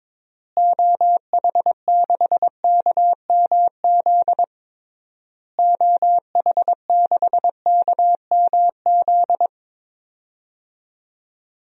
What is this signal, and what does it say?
Morse code, heard as a keyed tone, about 22 wpm, 700 Hz. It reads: OH6KMZ OH6KMZ
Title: DXCC-100 Call Signs - 6 Characters - Repeated 3x - 22wpm